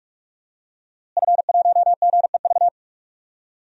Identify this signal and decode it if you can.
Morse code, heard as a keyed tone, about 46 wpm, 700 Hz. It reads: F1GEV
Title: DXCC-100 Call Signs - 5 Characters - Repeated 3x - 45wpm